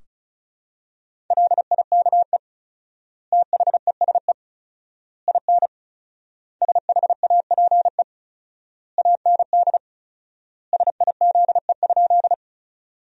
Morse code, heard as a keyed tone, about 36 wpm, 700 Hz. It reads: LIKE THESE IN SHAPE AND SIZE?
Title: 4x Word-Spacing - Sentences from Top 500 Words - 35wpm